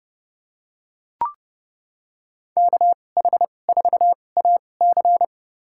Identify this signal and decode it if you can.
Morse code, heard as a keyed tone, about 30 wpm, spaced as for 22 wpm, 700 Hz. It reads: KH4AC
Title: USA Call Signs - 5 Characters - 2x Character Spacing - 30wpm